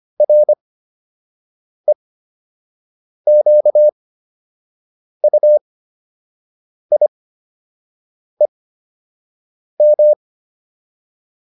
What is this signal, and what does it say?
Morse code, heard as a keyed tone, about 25 wpm, spaced as for 6 wpm, 600 Hz. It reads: REQUIEM
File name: RE7 06